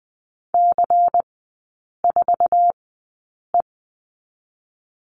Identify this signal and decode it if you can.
Morse code, heard as a keyed tone, about 20 wpm, spaced as for 8 wpm, 700 Hz. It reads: C4E
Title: 100 Lesson 17e - Letter u - Three Characters 20wpm